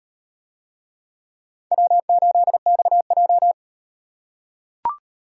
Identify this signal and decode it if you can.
Morse code, heard as a keyed tone, about 35 wpm, 700 Hz. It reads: W8XJ